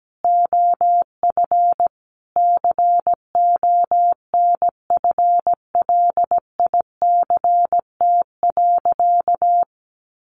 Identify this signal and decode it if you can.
Morse code, heard as a keyed tone, about 17 wpm, 700 Hz. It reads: OF CONFLICT.
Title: Sentences from Top 1400 Words 17wpm